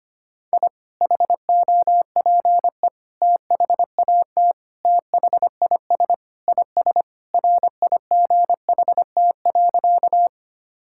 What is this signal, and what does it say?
Morse code, heard as a keyed tone, about 25 wpm, 700 Hz. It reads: I HOPE THAT THIS IS RIGHT.
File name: practice / 25wpm / Sentences with 3-Word Phrases - 25wpm